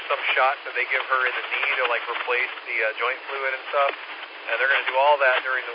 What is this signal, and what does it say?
Emergency Alert System (SAME) tones (header bursts), heard as harsh buzzing bursts, with speech. It is heard from 0.6 to 4.8 s.